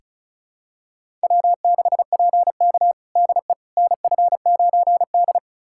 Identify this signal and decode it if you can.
Morse code, heard as a keyed tone, about 34 wpm, 700 Hz. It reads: W6PK DE NF9D
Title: USA Callsign DE Callsign - 4 Characters - 35wpm